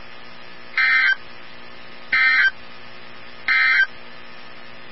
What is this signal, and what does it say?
Emergency Alert System (SAME) tones, heard as harsh buzzing bursts. It is heard from 0.7 to 3.9 s.
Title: NOAA Weather Radio 162.550